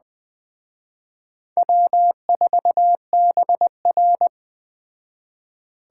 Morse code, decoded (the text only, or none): W4BR